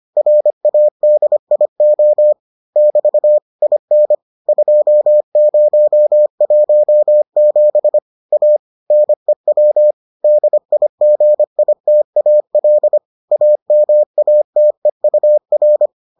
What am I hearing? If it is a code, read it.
Morse code, heard as a keyed tone, about 25 wpm, 600 Hz. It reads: RADIO <BT> IN 2017 A NEW DIGITAL AMATEUR